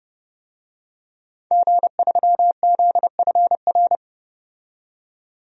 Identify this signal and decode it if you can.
Morse code, heard as a keyed tone, about 30 wpm, 700 Hz. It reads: G3ZFR
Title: DXCC-100 Call Signs - 5 Characters - 30wpm